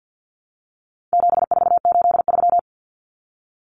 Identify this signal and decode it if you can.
Morse code, heard as a keyed tone, about 50 wpm, 700 Hz. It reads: 7483
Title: Common Cut Numbers - 4-Digit 50wpm